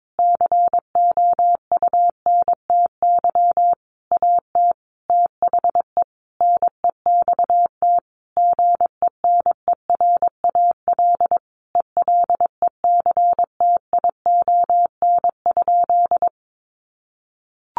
Morse code, heard as a keyed tone, about 22 wpm, 700 Hz. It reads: COUNTY AT THE NEXT GENERAL ELECTION?
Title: Sentences from Top 1500 Words 22wpm